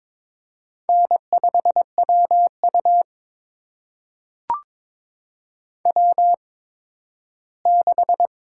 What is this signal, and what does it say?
Morse code, heard as a keyed tone, about 22 wpm, spaced as for 13 wpm, 700 Hz. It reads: N5WU W6
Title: USA Call Signs - 4 Characters - 8x Character Spacing - 22wpm